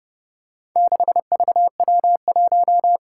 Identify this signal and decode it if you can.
Morse code, heard as a keyed tone, about 32 wpm, 700 Hz. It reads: 6VW1
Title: Four Challenging Letter-Number Combinations 30wpm